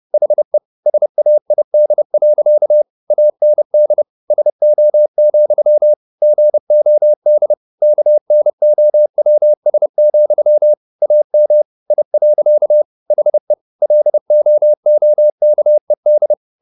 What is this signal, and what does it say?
Morse code, heard as a keyed tone, about 30 wpm, 600 Hz. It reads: HE SAID. AND SO, GOD KNOWS, AM I. HE LOOKED
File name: the old man and the sea / the old man and the sea 1 30wpm